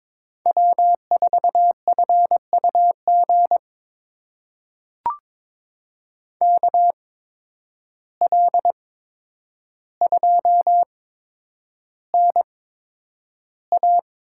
Morse code, heard as a keyed tone, about 21 wpm, spaced as for 9 wpm, 700 Hz. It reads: W4FUG KL2NA
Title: USA Call Signs - 8x Character Spacing - 22wpm